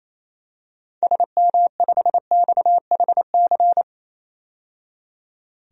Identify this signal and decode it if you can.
Morse code, heard as a keyed tone, about 28 wpm, 700 Hz. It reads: SM5XHC
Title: DXCC-100 Call Signs - 6 Characters - 28wpm